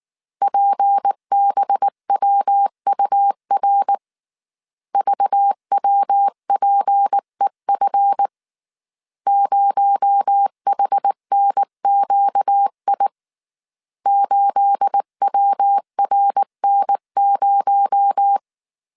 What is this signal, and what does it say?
Morse code, heard as a keyed tone, about 19 wpm, 800 Hz. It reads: PBWUR VWPEF 0HNQI 8WRN0
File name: CW / DET / 1 9 1500C S B 18wpm 2551